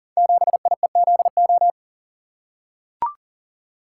Morse code, heard as a keyed tone, about 38 wpm, 700 Hz. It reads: 7IEZO
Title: Code Groups - 5 Letter:Numbers - 40wpm